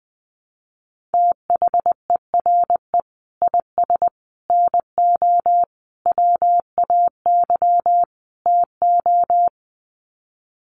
Morse code, heard as a keyed tone, about 20 wpm, 700 Hz. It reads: THERE IS NO WAY TO
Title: Sets of 5 Words Encore 20wpm